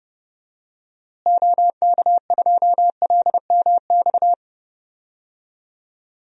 Morse code, heard as a keyed tone, about 31 wpm, 700 Hz. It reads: OK2LMX
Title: DXCC-100 Call Signs - 6 Characters - Repeated 3x - 30wpm